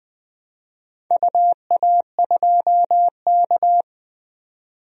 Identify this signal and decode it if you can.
Morse code, heard as a keyed tone, about 20 wpm, 700 Hz. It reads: UA2K
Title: DXCC-100 Call Signs - 4 Characters - 20wpm